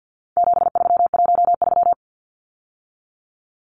Morse code, heard as a keyed tone, about 50 wpm, 700 Hz. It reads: 7313
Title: Numbers - 4-Digit 50wpm